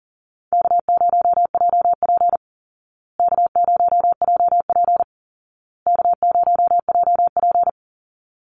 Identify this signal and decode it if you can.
Morse code, heard as a keyed tone, about 40 wpm, 700 Hz. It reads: K0JP K0JP K0JP
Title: Easy Call Signs Repeated 3x - 40wpm